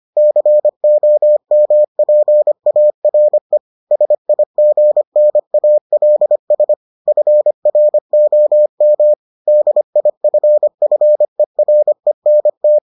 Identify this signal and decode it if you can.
Morse code, heard as a keyed tone, about 25 wpm, 600 Hz. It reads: COMPARE SIGNALS FROM DIFFERENT